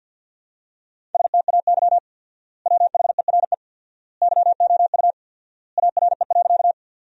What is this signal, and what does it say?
Morse code, heard as a keyed tone, about 50 wpm, 700 Hz. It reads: STAY WHERE YOU ARE.
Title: 4x Word-Spacing - Sentences from Top 500 Words - 50wpm